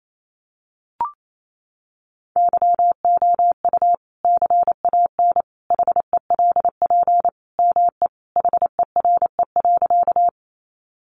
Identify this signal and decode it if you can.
Morse code, heard as a keyed tone, about 28 wpm, 700 Hz. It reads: YOU CAN HELP ME HERE.
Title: Sentences from Top 200 Words 28wpm